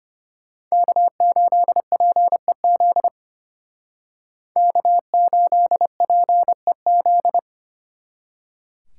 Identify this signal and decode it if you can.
Morse code, heard as a keyed tone, about 32 wpm, 700 Hz. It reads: K8PEZ K8PEZ